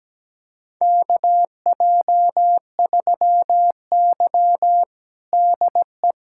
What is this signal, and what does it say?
Morse code, heard as a keyed tone, about 17 wpm, 700 Hz. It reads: KJ3Y DE
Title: USA Callsign DE Callsign - 4 Characters - 17wpm